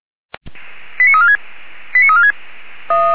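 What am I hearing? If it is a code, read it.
Five-tone selective call (ZVEI), heard as quick runs of five tones, tones of about 70 ms. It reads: 89346 89346
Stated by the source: Five-tone ZVEI selective call, received via SDR.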